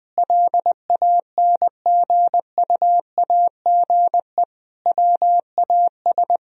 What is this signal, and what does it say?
Morse code, heard as a keyed tone, about 20 wpm, 700 Hz. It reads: LANGUAGE WAS
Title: Sentences from Top 1000 Words - Review Part 3 - 20wpm